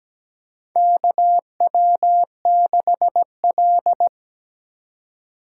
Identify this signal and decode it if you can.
Morse code, heard as a keyed tone, about 17 wpm, 700 Hz. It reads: KW6L
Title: USA Call Signs - 4 Characters - 2x Character Spacing - 17wpm